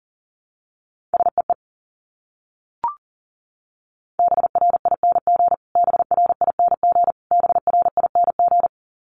Morse code, heard as a keyed tone, about 40 wpm, 700 Hz. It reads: SEE BRING BRING BRING
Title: Top 500 Words Repeated 40wpm